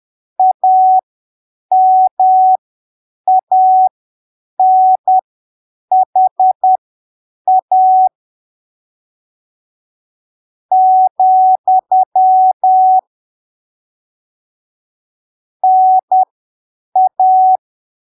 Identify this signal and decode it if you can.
Morse code, heard as a keyed tone, about 10 wpm, spaced as for 6 wpm, 750 Hz. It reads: AMANHA, NA
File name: arquivos / texto / texto 19 10ppm